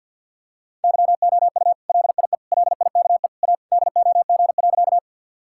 Morse code, heard as a keyed tone, about 50 wpm, 700 Hz. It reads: YOU LIE LIKE A DOG.
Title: Sentences from Top 1600 Words 50wpm